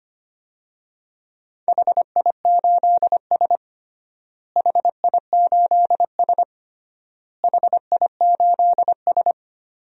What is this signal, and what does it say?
Morse code, heard as a keyed tone, about 25 wpm, 700 Hz. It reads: HI8S HI8S HI8S
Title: DXCC-100 Call Signs - 4 Characters - Repeated 3x - 25wpm